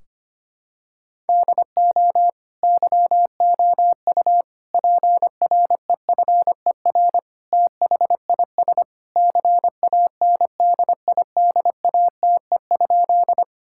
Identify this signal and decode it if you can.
Morse code, heard as a keyed tone, about 25 wpm, 700 Hz. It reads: DO YOU PREFER THIS CANDIDATE?
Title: Sentences from Top 1600 Words 25wpm